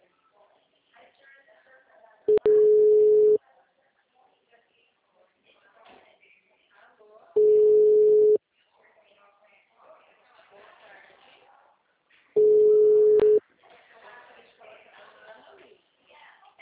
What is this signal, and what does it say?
Ringback tone (Europe, 425 Hz), with speech.